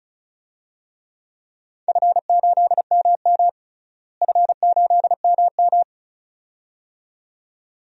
Morse code, heard as a keyed tone, about 34 wpm, 690 Hz. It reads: F8MM F8MM
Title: DXCC-100 Call Signs - 4 Characters - Repeated 2x - 35wpm